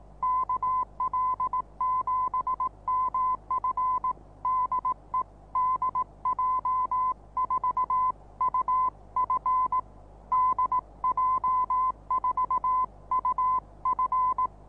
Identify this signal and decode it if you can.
Morse code, heard as a keyed tone, about 18 wpm, 1020 Hz. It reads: KL7MF DE DJ4UF DJ4UF
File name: repos / Morsekurs / DJ4UF Morsen26